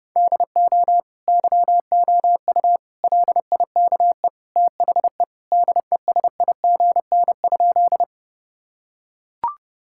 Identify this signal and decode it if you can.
Morse code, heard as a keyed tone, about 32 wpm, 700 Hz. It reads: DO YOU LIKE THE DESIGN?
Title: Sentences from Top 1000 Words - Review Part 5 - 30wpm